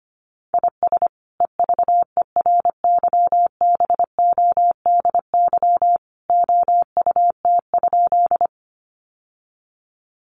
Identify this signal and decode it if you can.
Morse code, heard as a keyed tone, about 25 wpm, 700 Hz. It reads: IS EVERYBODY OUT?
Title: Sentences from Top 1400 Words 25wpm